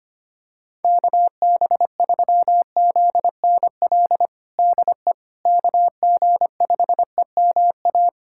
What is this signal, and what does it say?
Morse code, heard as a keyed tone, about 25 wpm, 700 Hz. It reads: KB3ZNL DE KG5EMA
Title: USA Callsign DE Callsign - 6 Characters - 25wpm